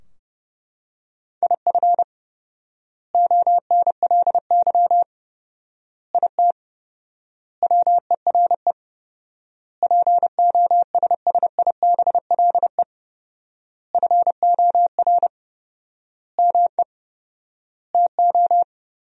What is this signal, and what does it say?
Morse code, heard as a keyed tone, about 30 wpm, 700 Hz. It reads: IF ONLY IT WERE POSSIBLE FOR ME TO
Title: 4x Word-Spacing - Sentences from Top 500 Words - 30wpm